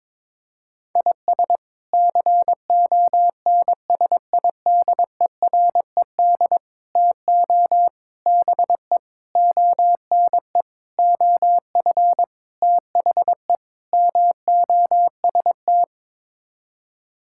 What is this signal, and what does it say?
Morse code, heard as a keyed tone, about 22 wpm, 700 Hz. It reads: IS CONSIDERED TO BE ONE OF THE MOST
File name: practice / 22wpm / Sets of 8 Words - 22wpm